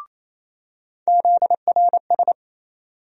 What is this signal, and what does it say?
Morse code, heard as a keyed tone, about 28 wpm, 700 Hz. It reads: ZRS
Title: USA Call Sign Postfixes - 3 Letters 28wpm